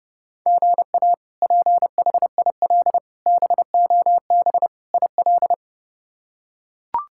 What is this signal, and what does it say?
Morse code, heard as a keyed tone, about 30 wpm, 700 Hz. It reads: GA PHIL BOB IL
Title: K1USN SST QSO Practice - 30wpm